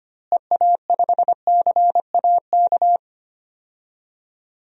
Morse code, heard as a keyed tone, about 25 wpm, 700 Hz. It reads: EA5CAK
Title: DXCC-100 Call Signs - 6 Characters - Repeated 3x - 25wpm